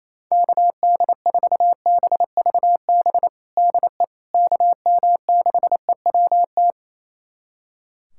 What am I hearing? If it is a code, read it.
Morse code, heard as a keyed tone, about 28 wpm, 700 Hz. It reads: KD4BVB DE KM6EWT
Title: USA Callsign DE Callsign - 6 Characters - 28wpm